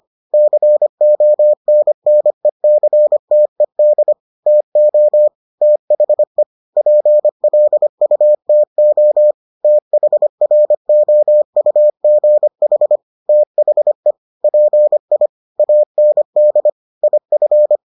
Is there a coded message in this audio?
Yes: Morse code from 0.3 to 17.8 s, heard as a keyed tone, about 25 wpm, 600 Hz. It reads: CONNECTED TO THE PLUTO THROUGH THE PI AND IF